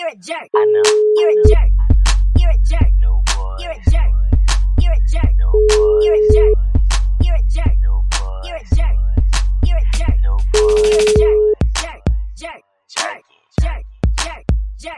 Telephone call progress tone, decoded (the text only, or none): ringback tone